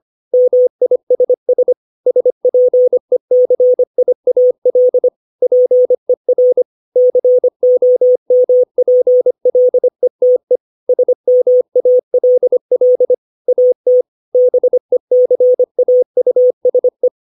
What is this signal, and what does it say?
Morse code, heard as a keyed tone, about 25 wpm, 500 Hz. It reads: MISS SPECIAL PER COMPLETE SMALL AT BECAUSE
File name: 632 words / 632 Words 25WPM